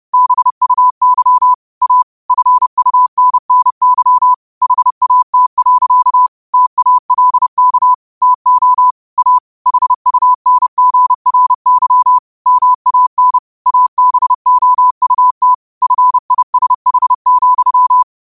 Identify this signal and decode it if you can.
Morse code, heard as a keyed tone, about 30 wpm, 1000 Hz. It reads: BUY A FUNNY HAT. TALK TO A HUNGRY MAN ABOUT FISH,
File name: qotdfiles / 30wpm